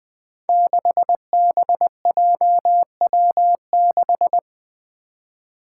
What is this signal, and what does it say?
Morse code, heard as a keyed tone, about 20 wpm, 700 Hz. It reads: 6BJW6
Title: Five Challenging Letter-Number Combinations 20wpm